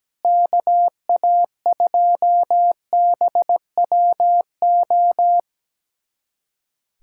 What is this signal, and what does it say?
Morse code, heard as a keyed tone, about 17 wpm, 700 Hz. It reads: KA2BWO
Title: USA Callsign DE Callsign - 6 Characters - 17wpm